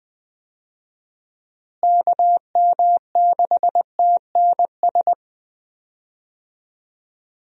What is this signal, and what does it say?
Morse code, heard as a keyed tone, about 20 wpm, 700 Hz. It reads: KM6TNS
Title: USA Call Signs - 6 Characters - Repeated 2x - 20wpm